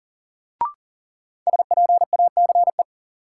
Morse code, heard as a keyed tone, about 40 wpm, 700 Hz. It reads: SPACE
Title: Sentences from Top 1000 Words - Review Part 5 - New Word Drill - 40wpm